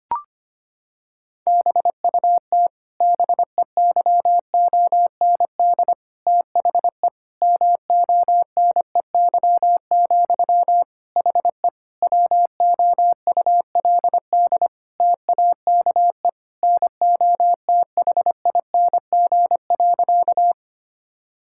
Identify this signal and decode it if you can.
Morse code, heard as a keyed tone, about 25 wpm, 700 Hz. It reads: BUT BEYOND THE MONEY, HE WOULD TAKE NOTHING.
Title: Sentences from Top 1000 Words - Review Part 6 - 25wpm